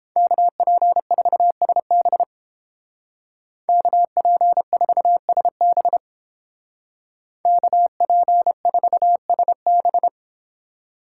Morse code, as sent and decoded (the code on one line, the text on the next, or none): -.- .--. ....- ... -... / -.- .--. ....- ... -... / -.- .--. ....- ... -...
KP4SB KP4SB KP4SB